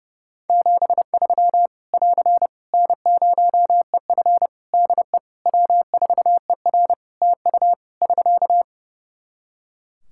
Morse code, heard as a keyed tone, about 32 wpm, 700 Hz. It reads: 73 <AR> N0EF DE W4ER TU <SK>
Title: Practice QSO - Exchange 4 - Ending - 30wpm